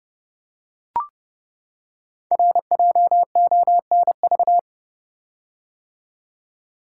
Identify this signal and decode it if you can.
Morse code, heard as a keyed tone, about 28 wpm, 700 Hz. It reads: RJONV
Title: Code Groups - 5 Letters - 30wpm